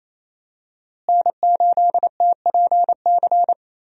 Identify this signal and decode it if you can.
Morse code, heard as a keyed tone, about 28 wpm, 700 Hz. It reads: N8TPC
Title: USA Call Signs - 5 Characters - Repeated 2x - 28wpm